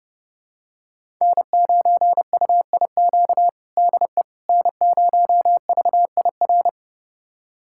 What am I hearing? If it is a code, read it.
Morse code, heard as a keyed tone, about 32 wpm, 700 Hz. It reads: N9UIQ DE N0VIR
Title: USA Callsign DE Callsign - 5 Characters - 30wpm